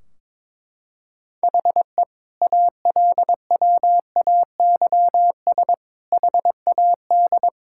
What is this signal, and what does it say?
Morse code, heard as a keyed tone, about 22 wpm, 700 Hz. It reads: HE ALWAYS HAD
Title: Sentences from Top 1600 Words 22wpm